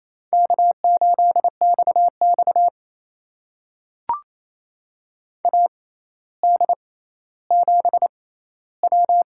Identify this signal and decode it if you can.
Morse code, heard as a keyed tone, about 28 wpm, spaced as for 14 wpm, 700 Hz. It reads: K8XX AD7W
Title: USA Call Signs - 4 Characters - 6x Character Spacing - 28wpm